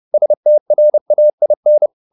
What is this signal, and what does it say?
Morse code, heard as a keyed tone, about 30 wpm, 600 Hz. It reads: STRAIN